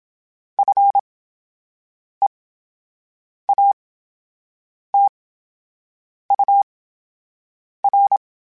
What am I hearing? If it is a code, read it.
Morse code, heard as a keyed tone, about 27 wpm, spaced as for 7 wpm, 800 Hz. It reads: FEATUR
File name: Special INGs / Special INGs-mp3s / VerySlowToFaster Ing @25